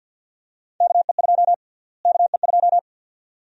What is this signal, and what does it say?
Morse code, heard as a keyed tone, about 52 wpm, 700 Hz. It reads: KE2 KE2
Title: USA Call Sign Prefixes - Repeated 2x - 50wpm